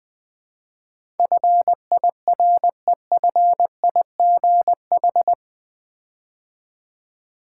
Morse code, heard as a keyed tone, about 20 wpm, 700 Hz. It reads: FIREFIGH
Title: Build A Word - 4 Letters At A Time - 11-Letter Words - 20wpm